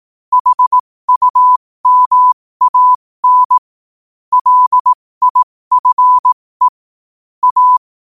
Morse code, heard as a keyed tone, about 18 wpm, spaced as for 15 wpm, 1000 Hz. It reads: HUMAN LIFE A
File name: qotdfiles / 15wpm